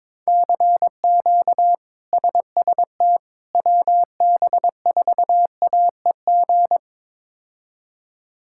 Morse code, heard as a keyed tone, about 21 wpm, 700 Hz. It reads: CQ SST WB4AEG
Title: K1USN SST QSO Practice - 22wpm